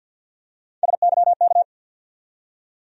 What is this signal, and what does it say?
Morse code, heard as a keyed tone, about 48 wpm, 700 Hz. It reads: SYK
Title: USA Call Sign Postfixes - 3 Letters 50wpm